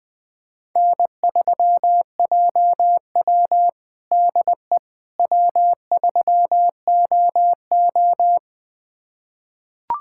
Morse code, heard as a keyed tone, about 20 wpm, 700 Hz. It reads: N3JW DE W3OO
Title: USA Callsign DE Callsign - 4 Characters - 20wpm